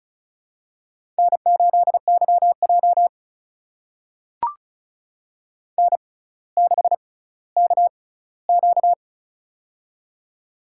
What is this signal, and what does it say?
Morse code, heard as a keyed tone, about 36 wpm, spaced as for 18 wpm, 700 Hz. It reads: N8YJ N6KQ
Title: USA Call Signs - 4 Characters - 6x Character Spacing - 35wpm